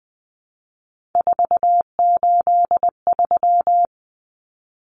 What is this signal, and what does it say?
Morse code, heard as a keyed tone, about 20 wpm, 700 Hz. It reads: 483
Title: Numbers - 3-Digit 20wpm